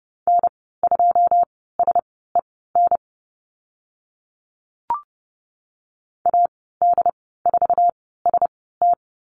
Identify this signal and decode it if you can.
Morse code, heard as a keyed tone, about 30 wpm, spaced as for 17 wpm, 700 Hz. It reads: N2SEN AD4ST
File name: Lesson 15g - Number 5 - Call Signs 30wpm